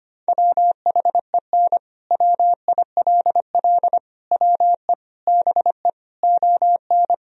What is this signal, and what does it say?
Morse code, heard as a keyed tone, about 25 wpm, 700 Hz. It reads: WHEN WILL WE BE ON
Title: Sentences from Top 1000 Words - Review Part 3 - 25wpm